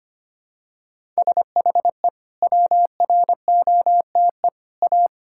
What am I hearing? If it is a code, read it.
Morse code, heard as a keyed tone, about 25 wpm, 700 Hz. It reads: SHE WROTE A
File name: practice / 25wpm / Sentences with 2-Word Phrases - 25wpm